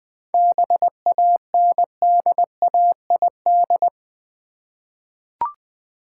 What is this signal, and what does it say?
Morse code, heard as a keyed tone, about 20 wpm, 700 Hz. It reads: BANDAID
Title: Easy Words with Hints - 20wpm